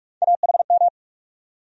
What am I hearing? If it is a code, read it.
Morse code, heard as a keyed tone, about 42 wpm, 700 Hz. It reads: AHM
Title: USA Call Sign Postfixes - 3 Letters 45wpm